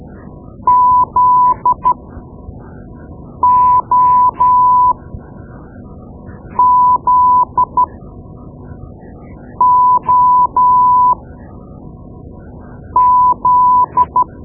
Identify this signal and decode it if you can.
Morse code, heard as a keyed tone, about 13 wpm, spaced as for 4 wpm, 1000 Hz. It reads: ZOZOZ
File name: Morse Lessons / Morse Lessons / Tape-2B